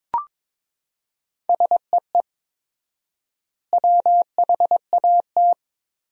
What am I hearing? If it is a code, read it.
Morse code, heard as a keyed tone, about 21 wpm, 700 Hz. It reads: SEE WHAT
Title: 4x Word-Spacing - Sentences from Top 500 Words - 22wpm